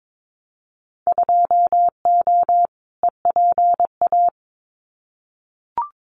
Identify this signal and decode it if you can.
Morse code, heard as a keyed tone, about 22 wpm, 700 Hz. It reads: 2O EPA
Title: Winter Field Day Exchanges Encore 22wpm